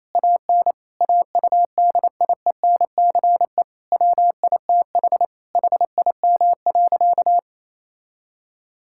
Morse code, heard as a keyed tone, about 28 wpm, 700 Hz. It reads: AN AUDIENCE WITH HIM.
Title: Sentences from Top 1000 Words - Review Part 5 - 28wpm